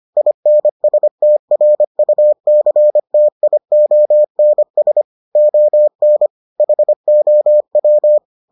Morse code, heard as a keyed tone, about 25 wpm, 600 Hz. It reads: INSTRUCTIONS ON HOW